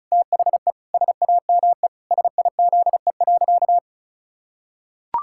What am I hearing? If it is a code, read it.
Morse code, heard as a keyed tone, about 34 wpm, 700 Hz. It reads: THE SAME SIZE.
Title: Pangram Challenge v2 - 35wpm